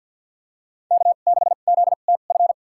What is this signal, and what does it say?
Morse code, heard as a keyed tone, about 53 wpm, spaced as for 37 wpm, 700 Hz. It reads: K6ZTF
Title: USA Call Signs - 5 Characters - 2x Character Spacing - 50wpm